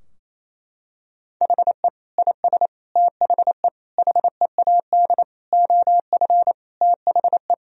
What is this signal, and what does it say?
Morse code, heard as a keyed tone, about 28 wpm, 700 Hz. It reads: HE IS THE HEAD OF THE
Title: Sentences from Top 1000 Words - Review Part 4 - 28wpm